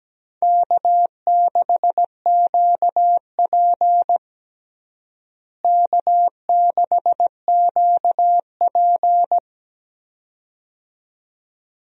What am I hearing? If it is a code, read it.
Morse code, heard as a keyed tone, about 17 wpm, 700 Hz. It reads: K6QP K6QP
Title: USA Call Signs - 4 Characters - Repeated 2x - 17wpm